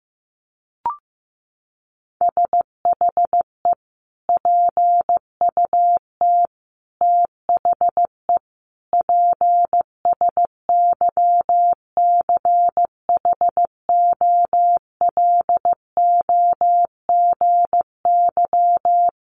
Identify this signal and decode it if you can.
Morse code, heard as a keyed tone, about 14 wpm, 700 Hz. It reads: SHE PUT THE PSYCHOLOGY
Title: Sentences from Top 1400 Words 15wpm